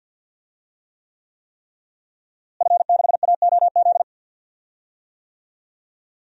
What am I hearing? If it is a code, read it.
Morse code, heard as a keyed tone, about 47 wpm, 700 Hz. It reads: F6AOZ
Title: DXCC-100 Call Signs - 5 Characters - Repeated 3x - 50wpm